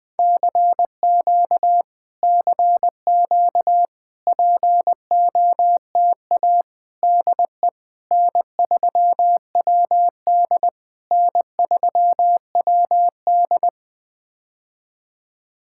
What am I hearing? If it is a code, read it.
Morse code, heard as a keyed tone, about 20 wpm, 700 Hz. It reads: CQ CQ POTA DE N3WD N3WD
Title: POTA QSO Practice - 20wpm